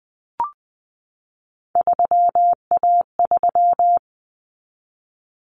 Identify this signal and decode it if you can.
Morse code, heard as a keyed tone, about 20 wpm, 700 Hz. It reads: 3A3
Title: Common Cut Numbers - 3-Digit 20wpm